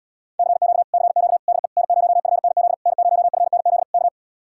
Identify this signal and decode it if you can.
Morse code, heard as a keyed tone, about 80 wpm, 700 Hz. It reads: CQ CQ DE N0FNZ N0FNZ K